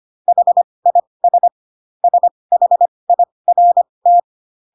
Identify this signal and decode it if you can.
Morse code, heard as a keyed tone, about 25 wpm, spaced as for 20 wpm, 700 Hz. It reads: HIS SHIRT
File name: Pinocchio / Pinocchio ch8 15 25 ik1hge